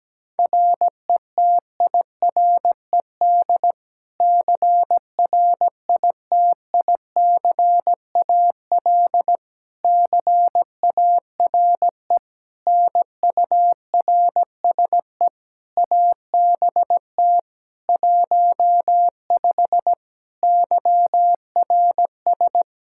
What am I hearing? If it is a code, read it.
Morse code, heard as a keyed tone, about 17 wpm, 700 Hz. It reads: RETIRED CRITICAL CARE NURSE ABT 15 YRS
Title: Practice QSO - Exchange 3 - More Info - 17wpm